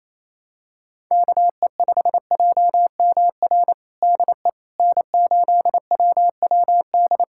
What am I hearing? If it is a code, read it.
Morse code, heard as a keyed tone, about 28 wpm, 700 Hz. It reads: KE5JMR DE N8WWD
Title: USA Callsign DE Callsign 28wpm